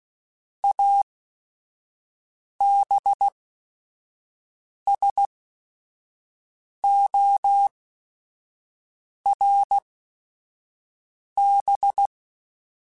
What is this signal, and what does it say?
Morse code, heard as a keyed tone, about 16 wpm, spaced as for 5 wpm, 800 Hz. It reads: ABSORB
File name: Technical Subjects / Technical Subjects-mp3s / XC @5